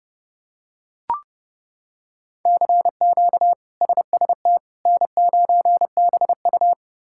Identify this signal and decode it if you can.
Morse code, heard as a keyed tone, about 30 wpm, 700 Hz. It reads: CQ SST N9BU
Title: K1USN SST QSO Practice - 30wpm